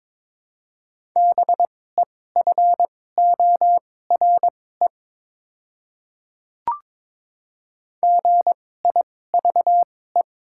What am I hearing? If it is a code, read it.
Morse code, heard as a keyed tone, about 22 wpm, spaced as for 16 wpm, 690 Hz. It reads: BEFORE GIVE
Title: 2x Character Spacing - Top 200 Words - 22wpm